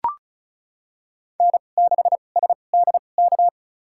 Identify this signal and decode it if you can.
Morse code, heard as a keyed tone, about 36 wpm, spaced as for 25 wpm, 700 Hz. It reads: N6SDK
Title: USA Call Signs - 2x Character Spacing - 35wpm